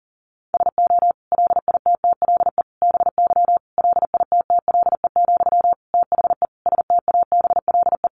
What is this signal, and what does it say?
Morse code, heard as a keyed tone, about 40 wpm, 700 Hz. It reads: SO LITTLE BY LITTLE, THE STABLE
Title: Sentences from Top 1500 Words 40wpm